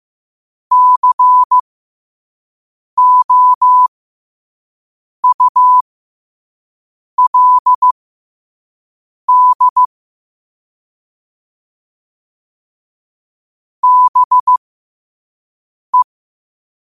Morse code, heard as a keyed tone, about 15 wpm, spaced as for 5 wpm, 1000 Hz. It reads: COULD BE